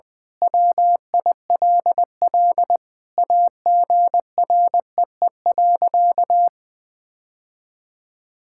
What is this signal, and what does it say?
Morse code, heard as a keyed tone, about 20 wpm, 700 Hz. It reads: WILL AGREE.
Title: Sentences with 5-Word Phrases - Spoken Beginning - 20wpm